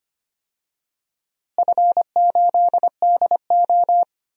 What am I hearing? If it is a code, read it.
Morse code, heard as a keyed tone, about 25 wpm, 700 Hz. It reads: F8DO
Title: DXCC-100 Call Signs - 4 Characters - Repeated 3x - 25wpm